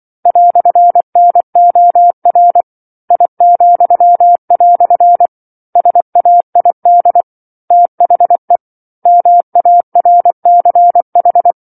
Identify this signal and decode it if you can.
Morse code, heard as a keyed tone, about 24 wpm, 700 Hz. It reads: "NOR I," SAID THE MARCH
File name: 07-Chapter-07 24 WPM